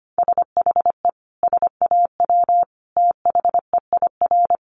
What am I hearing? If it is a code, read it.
Morse code, heard as a keyed tone, about 25 wpm, 700 Hz. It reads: SHE SAW THEIR
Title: Sentences from Top 1200 Words 25wpm